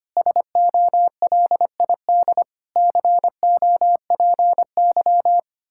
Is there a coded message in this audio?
Yes: Morse code from 0.2 to 5.4 s, heard as a keyed tone, about 32 wpm, 700 Hz. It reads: SOLID COPY